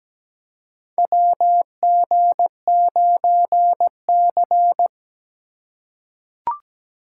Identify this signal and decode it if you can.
Morse code, heard as a keyed tone, about 17 wpm, 700 Hz. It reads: WG9C
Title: USA Call Signs - 4 Characters - 4x Character Spacing - 17wpm